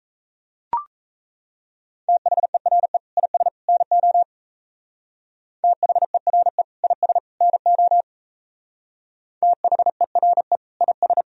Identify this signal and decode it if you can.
Morse code, heard as a keyed tone, about 34 wpm, 700 Hz. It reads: THERE IS NO THERE IS NO THERE IS
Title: Speed-Racing - Sets of 3 Words 28wpm